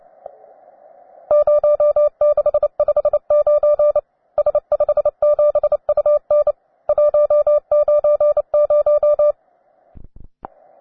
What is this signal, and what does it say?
Morse code, heard as a keyed tone, about 29 wpm, 590 Hz. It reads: 0659 S57UN 190
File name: wae / cw beginner / ua9am cw